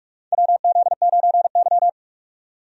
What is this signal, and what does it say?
Morse code, heard as a keyed tone, about 46 wpm, 700 Hz. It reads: WZ9Y
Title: USA Call Signs - 4 Characters - 45wpm